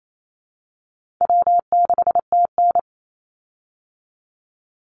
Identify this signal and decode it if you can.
Morse code, heard as a keyed tone, about 28 wpm, 700 Hz. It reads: W6TN
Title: Easy Call Signs Repeated 2x - 28wpm